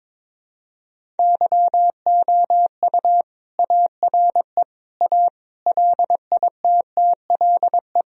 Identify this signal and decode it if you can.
Morse code, heard as a keyed tone, about 22 wpm, 700 Hz. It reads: YOU ARE A LITTLE
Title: Sentences from Top 1000 Words - Review Part 1 - 22wpm